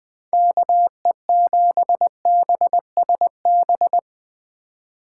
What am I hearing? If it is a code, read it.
Morse code, heard as a keyed tone, about 20 wpm, 700 Hz. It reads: KE7BSB
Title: Speed Racing - USA 2x3 Call Signs 17wpm